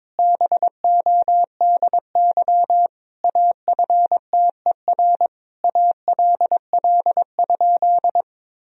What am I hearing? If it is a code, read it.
Morse code, heard as a keyed tone, about 22 wpm, 700 Hz. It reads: BODY AFTER ALL?
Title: Sentences from Top 1000 Words - Review Part 2 - 22wpm